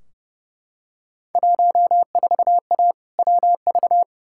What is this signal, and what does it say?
Morse code, heard as a keyed tone, about 30 wpm, 700 Hz. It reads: 14A WV
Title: Field Day Exchanges Encore - 30wpm